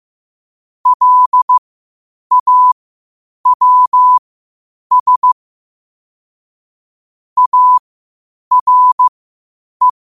Morse code, heard as a keyed tone, about 15 wpm, spaced as for 8 wpm, 1000 Hz. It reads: LAWS ARE
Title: 8 WPM morse code quotes for Thu, 29 Feb 2024